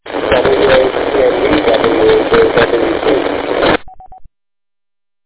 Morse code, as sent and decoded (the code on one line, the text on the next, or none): none